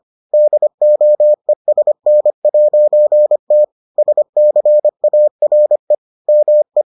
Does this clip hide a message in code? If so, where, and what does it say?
Yes: Morse code from 0.3 to 6.8 s, heard as a keyed tone, about 25 wpm, 600 Hz. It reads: DOESN'T SCARE ME